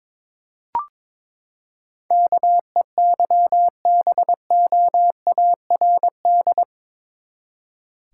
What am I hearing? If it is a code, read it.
Morse code, heard as a keyed tone, about 22 wpm, 700 Hz. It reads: KEYBOARD
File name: practice / 22wpm / Eight Letter Words - 22wpm